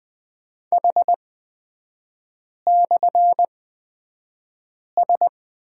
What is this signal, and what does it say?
Morse code, heard as a keyed tone, about 20 wpm, spaced as for 5 wpm, 700 Hz. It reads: H/S
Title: 224 Lesson 36e - BT - Three Characters 20wpm